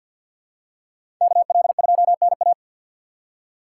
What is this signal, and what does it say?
Morse code, heard as a keyed tone, about 47 wpm, 690 Hz. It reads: KL2NA
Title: USA Call Signs - 5 Characters - 2x Character Spacing - 50wpm